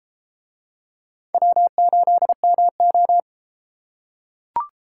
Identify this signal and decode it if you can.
Morse code, heard as a keyed tone, about 33 wpm, 700 Hz. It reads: W8MO